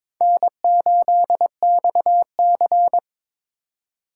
Morse code, heard as a keyed tone, about 22 wpm, 700 Hz. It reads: N8XC
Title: USA Call Signs - 4 Characters - 3x Character Spacing - 22wpm